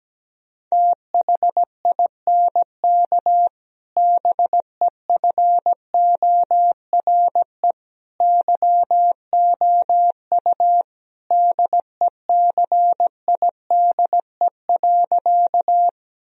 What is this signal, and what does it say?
Morse code, heard as a keyed tone, about 17 wpm, 700 Hz. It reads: THINK BEFORE YOU DECIDE.
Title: Sentences from Top 1000 Words - Review Part 3 - 17wpm